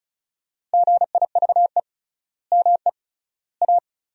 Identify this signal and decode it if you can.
Morse code, heard as a keyed tone, about 34 wpm, 700 Hz. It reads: GIVE ME A
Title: 3x Word-Spacing - Sentences from Top 200 Words - 35wpm